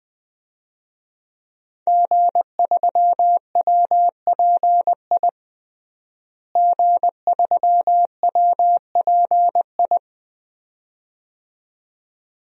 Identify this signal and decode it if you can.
Morse code, heard as a keyed tone, about 20 wpm, 700 Hz. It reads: G3WPI G3WPI
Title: DXCC-100 Call Signs - 5 Characters - Repeated 2x - 20wpm